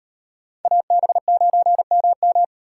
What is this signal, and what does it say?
Morse code, heard as a keyed tone, about 38 wpm, 700 Hz. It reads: AB9MM